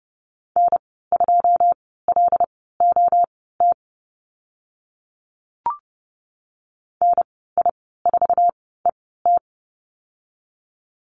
Morse code, heard as a keyed tone, about 30 wpm, spaced as for 17 wpm, 700 Hz. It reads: N2LOT NI4ET
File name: Lesson 15g - Number 5 - Call Signs 30wpm